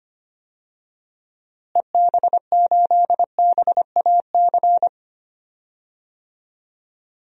Morse code, heard as a keyed tone, about 25 wpm, 700 Hz. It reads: EB8BAC
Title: DXCC-100 Call Signs - 6 Characters - Repeated 2x - 25wpm